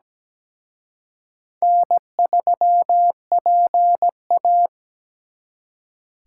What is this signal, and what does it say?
Morse code, heard as a keyed tone, about 20 wpm, 700 Hz. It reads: N3PA
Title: Speed Racing - USA 1x2 Call Signs 17wpm